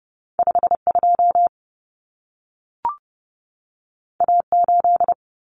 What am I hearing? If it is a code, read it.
Morse code, heard as a keyed tone, about 30 wpm, 700 Hz. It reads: 52 A8
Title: Common Cut Numbers - 2-Digit 30wpm